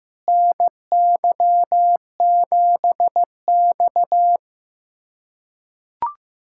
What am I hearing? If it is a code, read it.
Morse code, heard as a keyed tone, about 15 wpm, 700 Hz. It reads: NY7X
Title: USA Callsign DE Callsign - 4 Characters - 15wpm